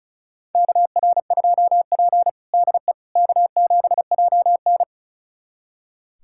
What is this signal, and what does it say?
Morse code, heard as a keyed tone, about 36 wpm, 700 Hz. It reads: KR2P DE K7JN
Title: USA Callsign DE Callsign - 4 Characters - 35wpm